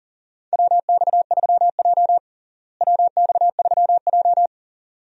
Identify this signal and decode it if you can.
Morse code, heard as a keyed tone, about 40 wpm, 700 Hz. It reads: WX3J WX3J
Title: USA Call Signs - 4 Characters - Repeated 2x - 40wpm